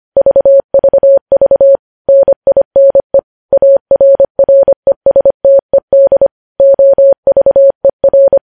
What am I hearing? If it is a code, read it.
Morse code, heard as a keyed tone, about 25 wpm, 550 Hz. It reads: VVV NINE ARRESTED OVER
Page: news headlines in morse code at 25 wpm